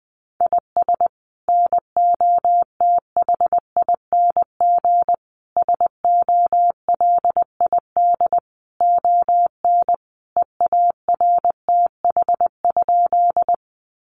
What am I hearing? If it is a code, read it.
Morse code, heard as a keyed tone, about 20 wpm, 700 Hz. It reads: IS NOTHING SOLID ON EARTH?
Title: Sentences from Top 1500 Words 20wpm